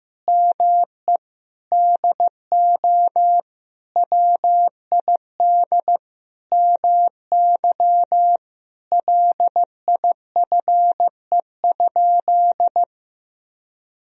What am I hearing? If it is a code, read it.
Morse code, heard as a keyed tone, about 15 wpm, 700 Hz. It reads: ME DO WID MY LIFE?
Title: Sentences from Top 300 Words and Abbreviations 15wpm